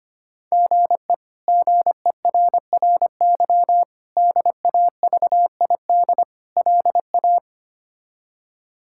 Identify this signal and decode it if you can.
Morse code, heard as a keyed tone, about 25 wpm, 700 Hz. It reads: GE GERRY DAVID LA
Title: K1USN SST QSO Practice - 25wpm